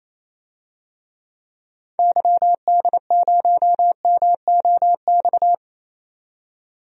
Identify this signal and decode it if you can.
Morse code, heard as a keyed tone, about 28 wpm, 700 Hz. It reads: YD0MOX
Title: DXCC-100 Call Signs - 6 Characters - 28wpm